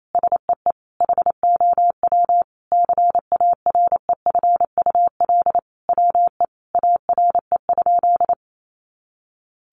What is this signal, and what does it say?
Morse code, heard as a keyed tone, about 28 wpm, 700 Hz. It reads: SEE HOW CAREFUL WE ARE?
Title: A Sound Of Thunder 28wpm